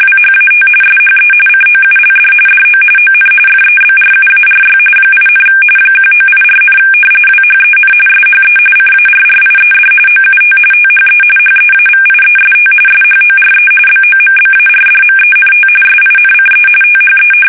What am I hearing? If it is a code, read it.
RTTY, about 45.45 baud. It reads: WHAT SHE COULD DO ON FEBRUARY 8, 1984, WHEN SHE CARRIED OUT THE HEAVIEST SHORE BOMBARDMENT SINCE THE
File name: fsk850 / battleshipnj